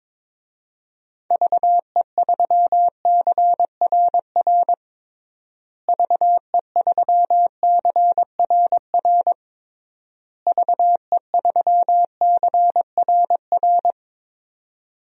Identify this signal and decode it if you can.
Morse code, heard as a keyed tone, about 22 wpm, 700 Hz. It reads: VE3CRR VE3CRR VE3CRR
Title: DXCC-100 Call Signs - 6 Characters - Repeated 3x - 22wpm